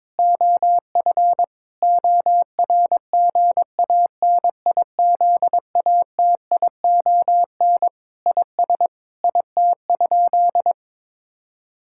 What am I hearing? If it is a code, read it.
Morse code, heard as a keyed tone, about 22 wpm, 700 Hz. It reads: OF ORGANIZATION IS IT?
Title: Sentences from Top 1000 Words - Review Part 5 - 22wpm